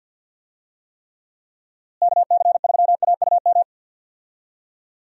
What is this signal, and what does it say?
Morse code, heard as a keyed tone, about 49 wpm, 700 Hz. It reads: KC3AUM
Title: USA Call Signs - 6 Characters - Repeated 2x - 50wpm